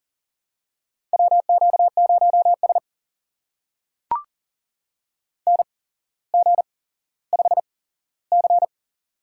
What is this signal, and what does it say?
Morse code, heard as a keyed tone, about 39 wpm, spaced as for 17 wpm, 700 Hz. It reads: WQ0S NG5C
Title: USA Call Signs - 4 Characters - 8x Character Spacing - 40wpm